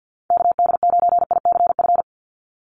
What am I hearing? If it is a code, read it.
Morse code, heard as a keyed tone, about 49 wpm, 700 Hz. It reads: KD9ICF
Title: USA Call Signs 50wpm